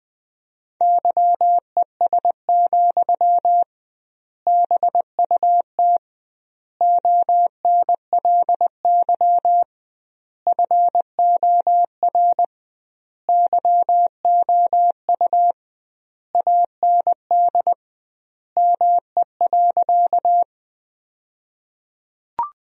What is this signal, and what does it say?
Morse code, heard as a keyed tone, about 21 wpm, 700 Hz. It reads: YES, BUT ONLY FOR YOU AND ME.
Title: 2x Word-Spacing - Sentences from Top 500 Words - 20wpm